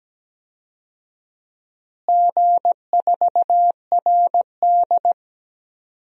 Morse code, heard as a keyed tone, about 17 wpm, 700 Hz. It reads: G4RD
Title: DXCC-100 Call Signs - 4 Characters - Repeated 3x - 17wpm